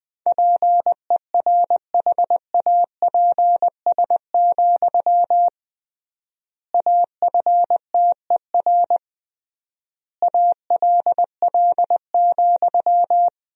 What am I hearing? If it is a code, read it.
Morse code, heard as a keyed tone, about 21 wpm, 700 Hz. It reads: PERHAPS, AFTER ALL,
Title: 3x Word-Spacing - Sentences from Top 500 Words - 20wpm